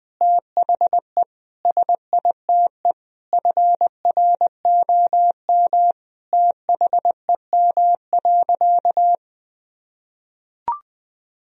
Morse code, heard as a keyed tone, about 20 wpm, 700 Hz. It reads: THE SITE FROM THEM.
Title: Sentences from Top 1000 Words - Review Part 4 - 20wpm